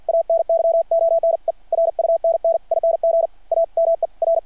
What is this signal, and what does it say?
Morse code, heard as a keyed tone, about 53 wpm, 660 Hz. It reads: ANYONE RUNNING A MER
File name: archived / CRSnet